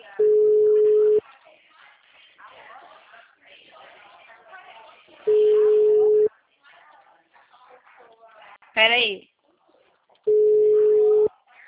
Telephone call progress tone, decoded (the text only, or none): ringback tone